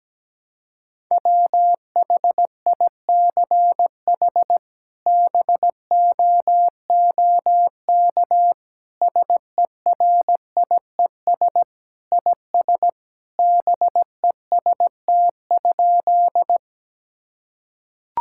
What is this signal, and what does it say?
Morse code, heard as a keyed tone, about 17 wpm, 700 Hz. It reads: WHICH BOOK SERIES IS BEST?
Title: Sentences from Top 1000 Words - Review Part 5 - 17wpm